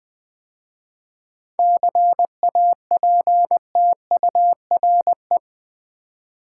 Morse code, heard as a keyed tone, about 20 wpm, 700 Hz. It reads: CAPTURE
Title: 7 Letter Words - First 3 Letters Given - 20wpm